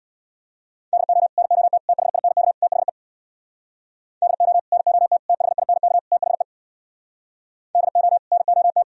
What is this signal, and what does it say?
Morse code, heard as a keyed tone, about 64 wpm, 700 Hz. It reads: DO NOT THINK THE DO NOT THINK THE DO NOT